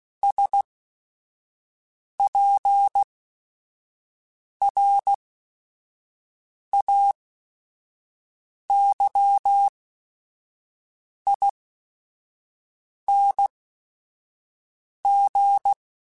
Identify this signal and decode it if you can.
Morse code, heard as a keyed tone, about 16 wpm, spaced as for 5 wpm, 800 Hz. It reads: SPRAYING